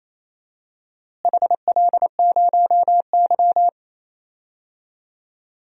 Morse code, heard as a keyed tone, about 28 wpm, 700 Hz. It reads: HL0Y
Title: DXCC-100 Call Signs - 4 Characters - Repeated 2x - 28wpm